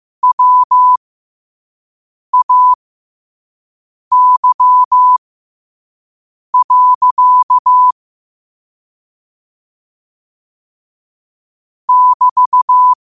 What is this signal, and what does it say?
Morse code, heard as a keyed tone, about 15 wpm, spaced as for 5 wpm, 1000 Hz. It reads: WAY. <BT>